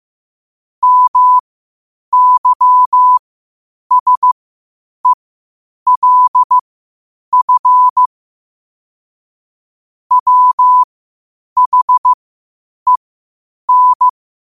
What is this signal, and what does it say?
Morse code, heard as a keyed tone, about 15 wpm, spaced as for 8 wpm, 1000 Hz. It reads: MYSELF WHEN
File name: qotdfiles / 8wpm